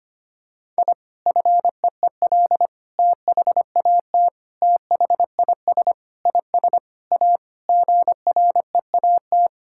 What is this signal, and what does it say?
Morse code, heard as a keyed tone, about 25 wpm, 700 Hz. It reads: I FEEL THAT THIS IS A GREAT
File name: practice / 25wpm / Sentences with 4-Word Phrases - 25wpm